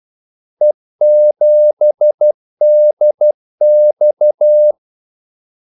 Morse code, heard as a keyed tone, about 12 wpm, 600 Hz. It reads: E7DX
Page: 2 ekstra ordmellemrum mellem kaldesignalerne